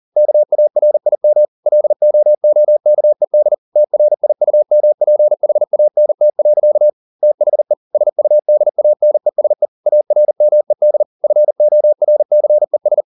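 Morse code, heard as a keyed tone, about 40 wpm, 600 Hz. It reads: KARIM LOOKED TRIUMPHANT. THE SUDANESE ARMED FORCES
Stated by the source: Morse code at 40 wpm, practice text, CW med Gnister.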